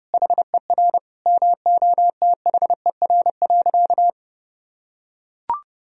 Morse code, heard as a keyed tone, about 30 wpm, 700 Hz. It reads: HER MOTHER.
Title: Sentences from Top 1000 Words - Review Part 4 - 30wpm